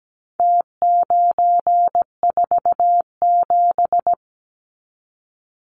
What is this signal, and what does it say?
Morse code, heard as a keyed tone, about 17 wpm, 700 Hz. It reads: T947
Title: Numbers - Contest Style - 4-Digit 17wpm